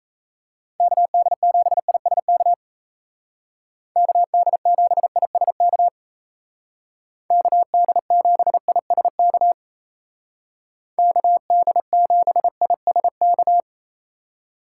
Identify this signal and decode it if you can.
Morse code, heard as a keyed tone, about 34 wpm, 700 Hz. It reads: KD7ISK KD7ISK KD7ISK KD7ISK
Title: Speed Racing - USA Call Signs 28wpm